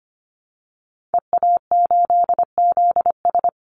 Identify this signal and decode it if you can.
Morse code, heard as a keyed tone, about 25 wpm, 700 Hz. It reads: EA8ZS
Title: Call Signs Repeated 2x - 25wpm